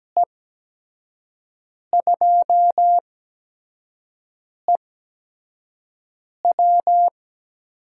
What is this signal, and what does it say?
Morse code, heard as a keyed tone, about 17 wpm, spaced as for 5 wpm, 700 Hz. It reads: E2EW
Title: USA Call Signs - 5 Characters - 8x Character Spacing - 17wpm